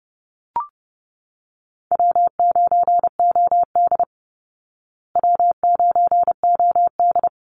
Morse code, heard as a keyed tone, about 30 wpm, 700 Hz. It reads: W9OD W9OD
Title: Call Signs Repeated 3x - 30wpm